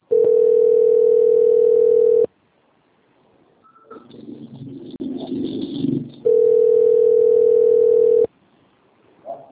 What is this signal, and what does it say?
Ringback tone (North America), 440 + 480 Hz.